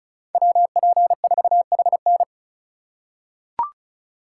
Morse code, heard as a keyed tone, about 34 wpm, 700 Hz. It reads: WP4HN
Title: USA Call Signs - 5 Characters - 3x Character Spacing - 35wpm